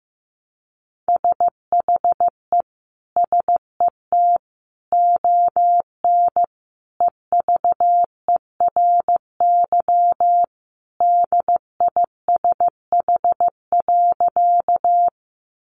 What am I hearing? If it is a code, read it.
Morse code, heard as a keyed tone, about 15 wpm, 700 Hz. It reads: SHE SET ON EVERY DISH.
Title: Sentences from Top 1300 Words 15wpm